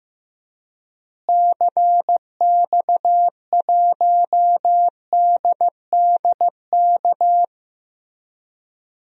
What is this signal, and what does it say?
Morse code, heard as a keyed tone, about 15 wpm, 700 Hz. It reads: CX1DDK
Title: DXCC-100 Call Signs - 6 Characters - Repeated 3x - 15wpm